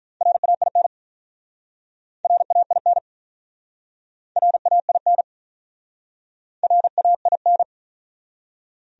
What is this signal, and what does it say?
Morse code, heard as a keyed tone, about 43 wpm, 700 Hz. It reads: RAIN RAIN RAIN RAIN
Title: Speed-Racing Top 500 Words 35wpm